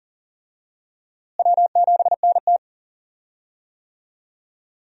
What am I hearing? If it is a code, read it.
Morse code, heard as a keyed tone, about 37 wpm, 700 Hz. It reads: W7NT
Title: USA Call Signs - 4 Characters - Repeated 3x - 40wpm